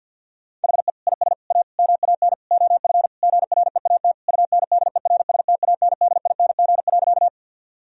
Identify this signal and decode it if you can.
Morse code, heard as a keyed tone, about 50 wpm, 700 Hz. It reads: HE IS A MAN OF GREAT UNDERSTANDING.
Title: Sentences from Top 1000 Words - Review Part 3 - 50wpm